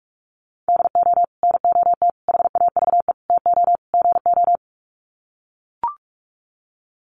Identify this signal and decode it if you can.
Morse code, heard as a keyed tone, about 44 wpm, 700 Hz. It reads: DO NOT HAVE TO GO
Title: Sets of 5 Words 45wpm